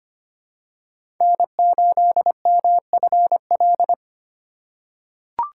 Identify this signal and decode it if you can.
Morse code, heard as a keyed tone, about 25 wpm, 700 Hz. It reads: N8MFL
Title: USA Call Signs - 3x Character Spacing - 25wpm